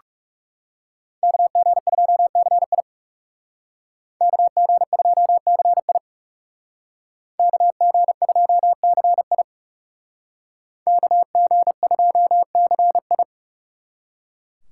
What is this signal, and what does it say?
Morse code, heard as a keyed tone, about 38 wpm, 700 Hz. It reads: KG2CI KG2CI KG2CI KG2CI